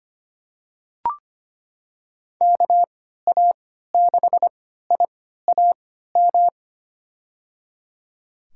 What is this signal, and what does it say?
Morse code, heard as a keyed tone, about 25 wpm, spaced as for 14 wpm, 700 Hz. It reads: KA6IAM
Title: USA Call Signs - 6 Characters - 3x Character Spacing - 25wpm